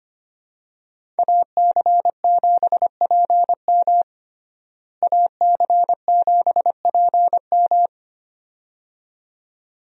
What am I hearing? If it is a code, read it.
Morse code, heard as a keyed tone, about 25 wpm, 700 Hz. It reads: AC7PM AC7PM
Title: USA Call Signs - 5 Characters - Repeated 2x - 25wpm